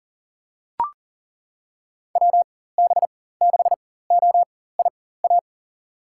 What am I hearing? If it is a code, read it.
Morse code, heard as a keyed tone, about 40 wpm, spaced as for 19 wpm, 700 Hz. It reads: WB6OIA
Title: USA Call Signs - 6 Characters - 4x Character Spacing - 40wpm